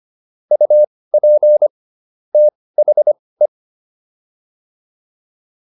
Morse code, heard as a keyed tone, about 25 wpm, spaced as for 18 wpm, 600 Hz. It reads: UP THE